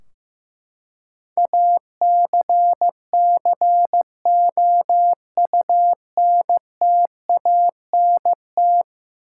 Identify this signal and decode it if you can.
Morse code, heard as a keyed tone, about 15 wpm, 700 Hz. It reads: ACCOUNTANT
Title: Practice QSO - Exchange 3 - More Info - 15wpm